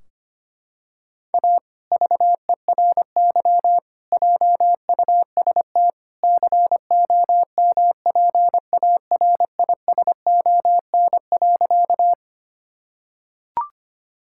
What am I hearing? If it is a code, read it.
Morse code, heard as a keyed tone, about 25 wpm, 700 Hz. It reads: A VERY JUST COMPARISON.
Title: Sentences from Top 1600 Words 25wpm